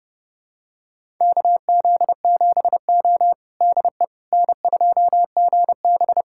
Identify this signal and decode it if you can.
Morse code, heard as a keyed tone, about 29 wpm, 700 Hz. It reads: KZ7O DE N2GB
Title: USA Callsign DE Callsign - 4 Characters - 30wpm